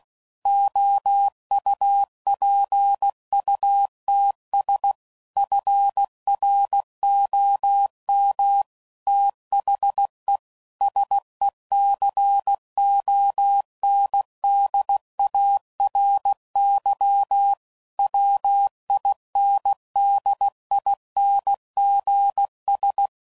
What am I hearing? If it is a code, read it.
Morse code, heard as a keyed tone, about 16 wpm, 800 Hz. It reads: OUPUTS FROM THE SECONDARY WINDINGS
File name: CW Code Groups @15WPM / Technical Subjects / PwrSupply